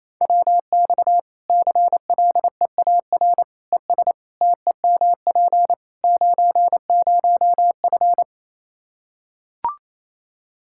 Morse code, heard as a keyed tone, about 28 wpm, 700 Hz. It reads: WX CLEAR ES TEMP 90F
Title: Practice QSO - Exchange 2 - Info - 28wpm